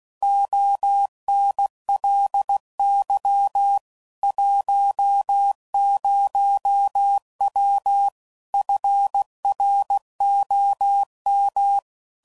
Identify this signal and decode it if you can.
Morse code, heard as a keyed tone, about 16 wpm, 800 Hz. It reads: ONLY 10W FROM
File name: CW Code Groups @15WPM / QSOs / kc7sac